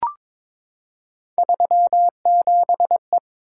Morse code, heard as a keyed tone, about 22 wpm, 700 Hz. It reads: 37E
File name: practice / 22wpm / Three Letter-Number Combinations - 22wpm